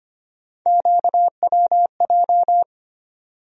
Morse code, heard as a keyed tone, about 25 wpm, 700 Hz. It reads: QWJ
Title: USA Call Sign Postfixes - 3 Letters 25wpm